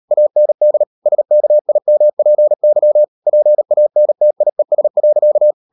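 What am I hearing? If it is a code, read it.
Morse code, heard as a keyed tone, about 38 wpm, 600 Hz. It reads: AND SKIMPY PANTIES.
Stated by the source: Morse code practice at 38 WPM, sending Ken Follett's novel Never in English, from CW med Gnister.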